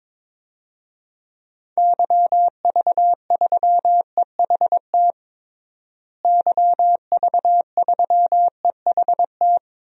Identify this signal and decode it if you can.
Morse code, heard as a keyed tone, about 22 wpm, 700 Hz. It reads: YV3EHT YV3EHT
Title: DXCC-100 Call Signs - 6 Characters - Repeated 3x - 22wpm